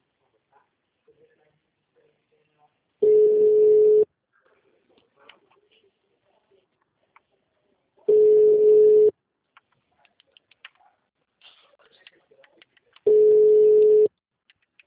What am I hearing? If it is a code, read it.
Ringback tone (Europe, 425 Hz), with speech.